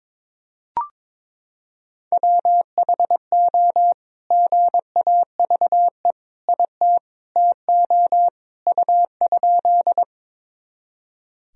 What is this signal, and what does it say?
Morse code, heard as a keyed tone, about 22 wpm, 700 Hz. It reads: WHO GAVE IT TO U?
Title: Sentences from Top 500 Words and Abbreviations 22wpm